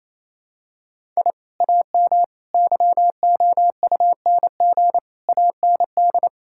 Morse code, heard as a keyed tone, about 28 wpm, 700 Hz. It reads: I AM YOUNG AND
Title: Sentences from Top 1000 Words - Review Part 5 - 28wpm